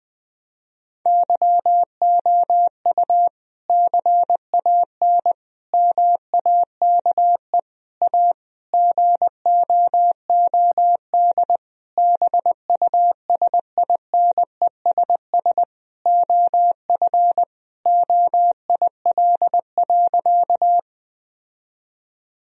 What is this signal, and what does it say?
Morse code, heard as a keyed tone, about 20 wpm, 700 Hz. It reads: YOU CAN MAKE A GOOD BUSINESS OF OIL.
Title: Sentences from Top 1000 Words - Review Part 4 - 20wpm